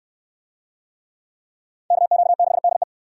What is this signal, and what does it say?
Morse code, heard as a keyed tone, about 60 wpm, 700 Hz. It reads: KQ6DE